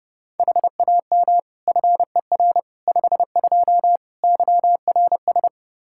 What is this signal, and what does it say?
Morse code, heard as a keyed tone, about 30 wpm, 700 Hz. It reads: HAM FER 52 YRS
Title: Practice QSO - Exchange 3 - More Info - 30wpm